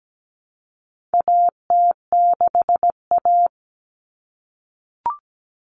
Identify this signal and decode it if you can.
Morse code, heard as a keyed tone, about 17 wpm, 700 Hz. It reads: AT6A
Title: Common Cut Numbers - 4-Digit 17wpm